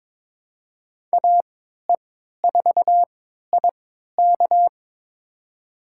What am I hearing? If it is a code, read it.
Morse code, heard as a keyed tone, about 22 wpm, spaced as for 12 wpm, 700 Hz. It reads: AE4IK
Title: USA Call Signs - 5 Characters - 3x Character Spacing - 22wpm